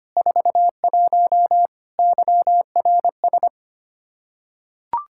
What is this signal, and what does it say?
Morse code, heard as a keyed tone, about 25 wpm, 700 Hz. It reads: 41 YRS
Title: Practice QSO - Exchange 3 - More Info - 25wpm